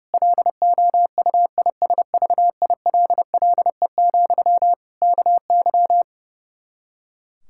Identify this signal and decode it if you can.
Morse code, heard as a keyed tone, about 30 wpm, 700 Hz. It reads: LOUISVILLE, KY
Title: K1USN SST QSO Practice 30wpm